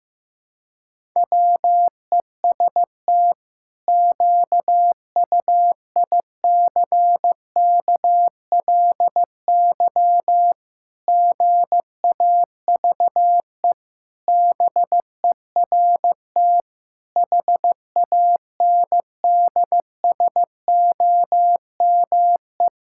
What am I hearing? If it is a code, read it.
Morse code, heard as a keyed tone, about 16 wpm, 700 Hz. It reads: WEST QUICKLY GAVE BERT HANDSOME
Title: Pangram Challenge v2 - 15wpm